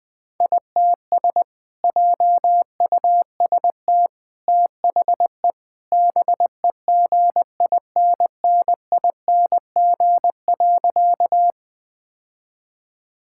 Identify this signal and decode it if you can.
Morse code, heard as a keyed tone, about 20 wpm, 700 Hz. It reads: ITS JUST THE BEGINNING.
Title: Sentences from Top 1000 Words - Review Part 5 - 20wpm